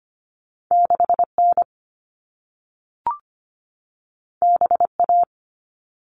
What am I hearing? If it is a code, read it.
Morse code, heard as a keyed tone, about 25 wpm, 700 Hz. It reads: 6N BA
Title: All Cut Numbers - 2-Digit 25wpm